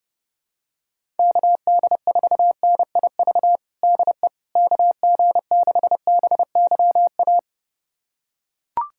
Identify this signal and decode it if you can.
Morse code, heard as a keyed tone, about 30 wpm, 700 Hz. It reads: KD4NIV DE KG6BYA
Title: USA Callsign DE Callsign 30wpm